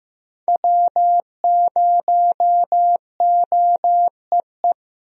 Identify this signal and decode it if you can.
Morse code, heard as a keyed tone, about 16 wpm, 700 Hz. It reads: W0OEE
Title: USA Call Signs - 5 Characters - Repeated 3x - 15wpm